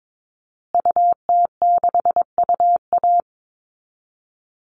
Morse code, heard as a keyed tone, about 22 wpm, 700 Hz. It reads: UT6UA
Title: Call Signs Repeated 2x - 22wpm